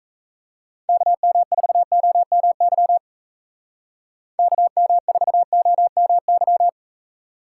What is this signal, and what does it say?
Morse code, heard as a keyed tone, about 33 wpm, 700 Hz. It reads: KM4OMY KM4OMY
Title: Speed Racing - USA 2x3 Call Signs 28wpm